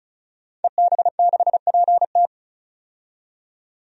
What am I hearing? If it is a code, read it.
Morse code, heard as a keyed tone, about 36 wpm, 700 Hz. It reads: EB6PT
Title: DXCC-100 Call Signs - 5 Characters - 35wpm